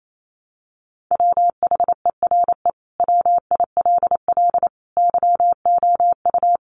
Morse code, heard as a keyed tone, about 28 wpm, 700 Hz. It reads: WHERE WILL YOU
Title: Sentences from Top 1300 Words 28wpm